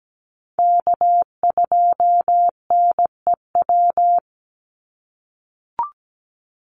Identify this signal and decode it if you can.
Morse code, heard as a keyed tone, about 17 wpm, 700 Hz. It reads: K2NEW
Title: USA Call Signs 17wpm